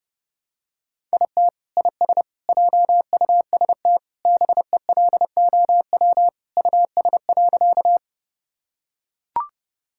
Morse code, heard as a keyed tone, about 30 wpm, 700 Hz. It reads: IT IS JUST BELOW US.
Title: Sentences from Top 1000 Words - Review Part 5 - 30wpm